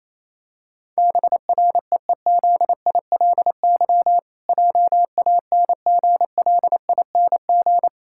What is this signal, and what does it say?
Morse code, heard as a keyed tone, about 28 wpm, 700 Hz. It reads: BREEZILY JANGLING
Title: Pangram Challenge v2 - 28wpm